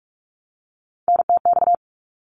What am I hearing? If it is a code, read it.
Morse code, heard as a keyed tone, about 46 wpm, 700 Hz. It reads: NTX
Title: ARRL Sections 45wpm